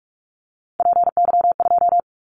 Morse code, heard as a keyed tone, about 46 wpm, 700 Hz. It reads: PY2
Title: Callsign Prefixes for Top 100 DXCC - 45wpm